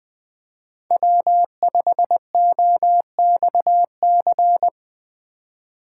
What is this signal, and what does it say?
Morse code, heard as a keyed tone, about 20 wpm, 700 Hz. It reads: W5OXC